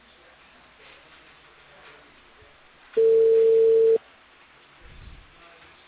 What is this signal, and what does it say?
Telephone call progress tone (China, 450 Hz), heard as a telephone network's tone, about 450 Hz. Ringback tone.